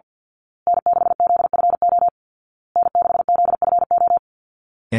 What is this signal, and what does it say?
Morse code, heard as a keyed tone, about 50 wpm, 700 Hz. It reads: N6ZFO N6ZFO
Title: Call Signs Repeated 2x - 50wpm